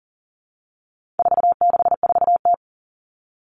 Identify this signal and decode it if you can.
Morse code, heard as a keyed tone, about 40 wpm, 700 Hz. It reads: 464T
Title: All Cut Numbers - 4-Digit 40wpm